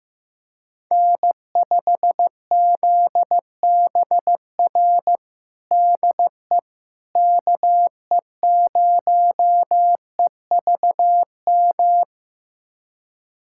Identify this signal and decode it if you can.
Morse code, heard as a keyed tone, about 16 wpm, 700 Hz. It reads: N5ZBR DE KE0EVM
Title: USA Callsign DE Callsign 15wpm